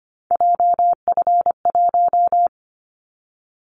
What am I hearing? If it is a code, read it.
Morse code, heard as a keyed tone, about 25 wpm, 700 Hz. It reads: JF1
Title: Callsign Prefixes for Top 25 DXCC - 25wpm